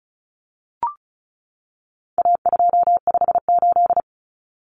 Morse code, heard as a keyed tone, about 34 wpm, 700 Hz. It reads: A258
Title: Common Cut Numbers - 4-Digit 35wpm